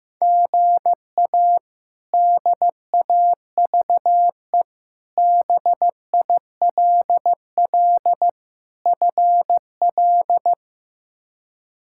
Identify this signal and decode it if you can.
Morse code, heard as a keyed tone, about 15 wpm, 700 Hz. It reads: GA DAVE BILL FL
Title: K1USN SST QSO Practice - 15wpm